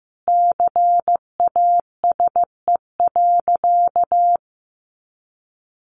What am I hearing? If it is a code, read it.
Morse code, heard as a keyed tone, about 15 wpm, 700 Hz. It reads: CASE.
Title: Sentences from Top 1200 Words 15wpm